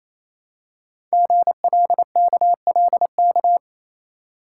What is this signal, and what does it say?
Morse code, heard as a keyed tone, about 28 wpm, 700 Hz. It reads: GLKLK
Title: Code Groups - 5 Letters - 28wpm